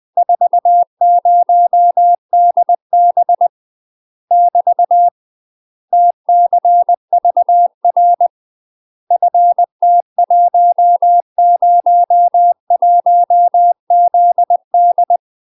Morse code, heard as a keyed tone, about 20 wpm, 700 Hz. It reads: 40DB <BT> TCVR FT101ZD
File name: UK MASTER QSOs TRAINING/20WPM